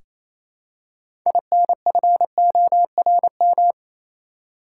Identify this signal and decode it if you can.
Morse code, heard as a keyed tone, about 28 wpm, 700 Hz. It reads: INFORM
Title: Six Letter Words - 28wpm